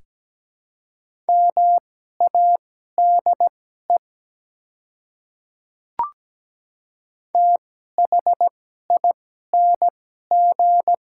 Morse code, heard as a keyed tone, about 17 wpm, spaced as for 12 wpm, 700 Hz. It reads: MADE THING
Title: 2x Character Spacing - Top 300 Words - 17wpm